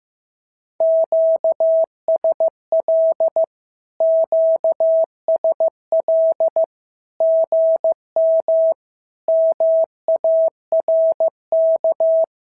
Morse code, heard as a keyed tone, about 15 wpm, 650 Hz. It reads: QSL QSL GM MARK
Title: SOTA QSO Practice 15wpm